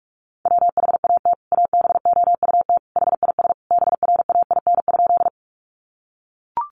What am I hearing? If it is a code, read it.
Morse code, heard as a keyed tone, about 44 wpm, 700 Hz. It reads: WHAT ABOUT HIS BRAIN?
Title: Sentences from Top 1100 Words 45wpm